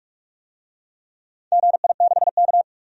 Morse code, heard as a keyed tone, about 47 wpm, 700 Hz. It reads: GI6K
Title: DXCC-100 Call Signs - 4 Characters - 45wpm